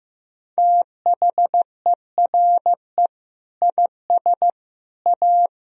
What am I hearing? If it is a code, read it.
Morse code, heard as a keyed tone, about 15 wpm, 700 Hz. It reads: THERE IS A
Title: Sentences from Top 1000 Words - Review Part 4 - 15wpm